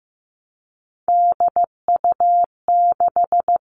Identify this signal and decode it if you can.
Morse code, heard as a keyed tone, about 15 wpm, 700 Hz. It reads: DU6
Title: Three Challenging Letter-Number Combinations 15wpm